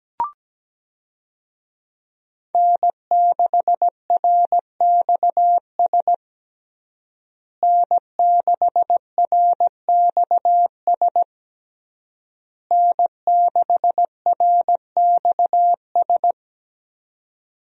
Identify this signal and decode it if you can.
Morse code, heard as a keyed tone, about 17 wpm, 700 Hz. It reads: N6RXS N6RXS N6RXS
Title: USA Call Signs - 5 Characters - Repeated 3x - 17wpm